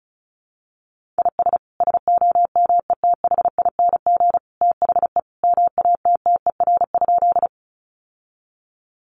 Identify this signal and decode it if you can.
Morse code, heard as a keyed tone, about 34 wpm, 700 Hz. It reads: IS SOMETHING THE MATTER?
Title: Sentences from Top 700 Words 35wpm